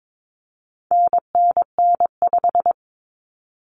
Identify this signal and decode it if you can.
Morse code, heard as a keyed tone, about 22 wpm, 700 Hz. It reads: NNN5
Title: Common Cut Numbers - 4-Digit 22wpm